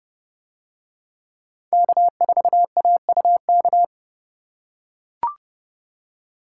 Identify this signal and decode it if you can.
Morse code, heard as a keyed tone, about 29 wpm, 700 Hz. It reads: K4AUK